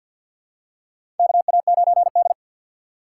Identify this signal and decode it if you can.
Morse code, heard as a keyed tone, about 47 wpm, 700 Hz. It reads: KA9D
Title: USA Call Signs - 4 Characters - 3x Character Spacing - 50wpm